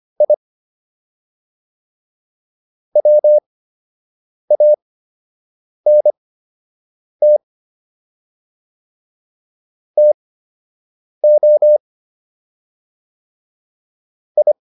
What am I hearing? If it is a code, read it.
Morse code, heard as a keyed tone, about 25 wpm, spaced as for 7 wpm, 600 Hz. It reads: I WANT TO I